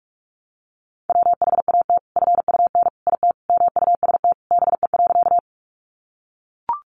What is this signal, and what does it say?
Morse code, heard as a keyed tone, about 44 wpm, 700 Hz. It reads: WHAT FUN IT MUST BE.
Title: Sentences from Top 700 Words 45wpm